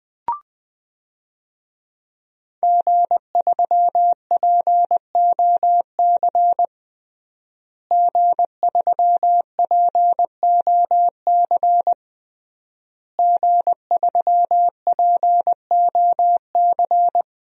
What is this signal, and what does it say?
Morse code, heard as a keyed tone, about 20 wpm, 700 Hz. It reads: G3POC G3POC G3POC
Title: DXCC-100 Call Signs - 5 Characters - Repeated 3x - 20wpm